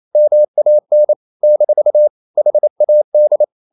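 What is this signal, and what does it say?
Morse code, heard as a keyed tone, about 28 wpm, 600 Hz. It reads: MAN - HAD
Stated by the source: Morse code practice at 28 wpm, text of Ken Follett's novel Never in English, from CW med Gnister.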